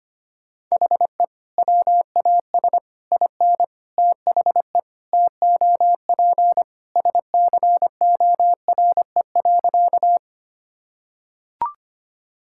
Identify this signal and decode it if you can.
Morse code, heard as a keyed tone, about 25 wpm, 700 Hz. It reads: HE WAS IN THE TOP SCORE.
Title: Sentences from Top 1600 Words 25wpm